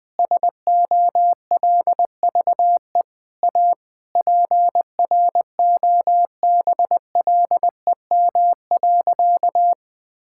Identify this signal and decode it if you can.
Morse code, heard as a keyed tone, about 21 wpm, 700 Hz. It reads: SOLVE A PROBLEM.
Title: Sentences with 9-Word Phrases - 20wpm